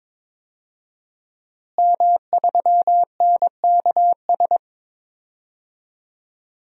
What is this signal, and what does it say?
Morse code, heard as a keyed tone, about 22 wpm, 700 Hz. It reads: M3NKS
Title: DXCC-100 Call Signs - 5 Characters - Repeated 3x - 22wpm